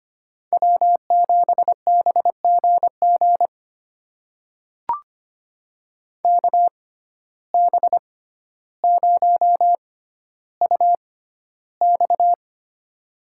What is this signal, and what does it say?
Morse code, heard as a keyed tone, about 25 wpm, spaced as for 13 wpm, 700 Hz. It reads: W7BGG KB0UX
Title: USA Call Signs - 5 Characters - 6x Character Spacing - 25wpm